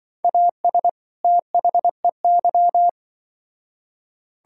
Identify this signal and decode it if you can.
Morse code, heard as a keyed tone, about 25 wpm, 700 Hz. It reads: AS THEY
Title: Speed-Racing - Sets of 2 Words 20wpm